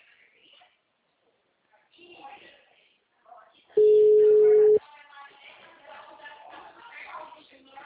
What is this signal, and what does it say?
Ringback tone (Europe, 425 Hz), with speech.